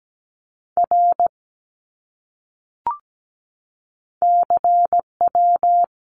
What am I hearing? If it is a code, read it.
Morse code, heard as a keyed tone, about 17 wpm, 700 Hz. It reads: R CW
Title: Abbreviations Repeated 3x - 17wpm